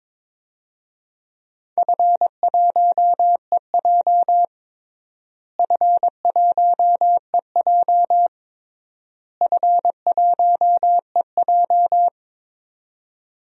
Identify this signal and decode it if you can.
Morse code, heard as a keyed tone, about 22 wpm, 700 Hz. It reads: F1EJ F1EJ F1EJ
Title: DXCC-100 Call Signs - 4 Characters - Repeated 3x - 22wpm